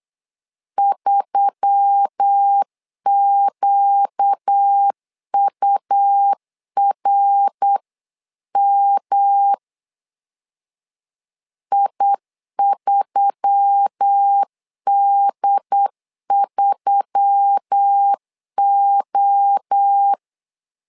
Morse code, heard as a keyed tone, about 8 wpm, 800 Hz. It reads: 3QURM I3D3O
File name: LEZIONI / 4 Q1S3 8wpm 3037